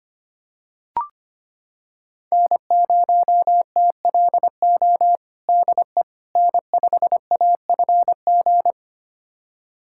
Morse code, heard as a keyed tone, about 25 wpm, 700 Hz. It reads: N0TLO DE N5AFG
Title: USA Callsign DE Callsign - 5 Characters - 25wpm